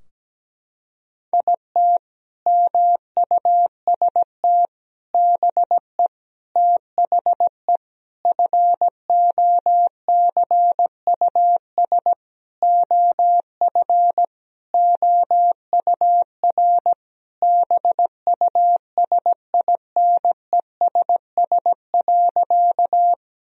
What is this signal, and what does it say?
Morse code, heard as a keyed tone, about 17 wpm, 700 Hz. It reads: IT MUST BE THE FOCUS OF OUR BUSINESS.
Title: Sentences from Top 1000 Words - Review Part 4 - 17wpm